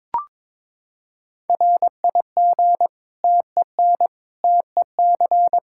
Morse code, heard as a keyed tone, about 21 wpm, 700 Hz. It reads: RIG TEN TEC
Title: Practice QSO - Exchange 2 - Info - 22wpm